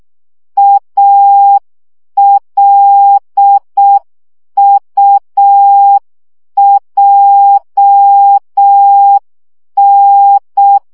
Morse code, heard as a keyed tone, about 6 wpm, 800 Hz. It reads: ALUJN
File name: morze vizsga 2011-2 5